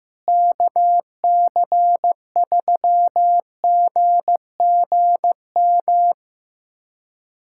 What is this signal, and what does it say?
Morse code, heard as a keyed tone, about 15 wpm, 700 Hz. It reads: KC3GGM
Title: USA Callsign DE Callsign - 6 Characters - 15wpm